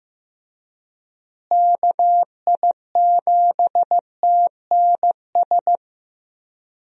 Morse code, heard as a keyed tone, about 15 wpm, 700 Hz. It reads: KI7TNS
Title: USA Call Signs - 6 Characters - Repeated 3x - 15wpm